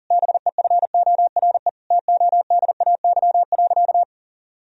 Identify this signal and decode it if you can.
Morse code, heard as a keyed tone, about 40 wpm, 700 Hz. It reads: BEFORE TODAY.
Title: Sentences from Top 1000 Words - Review Part 4 - 40wpm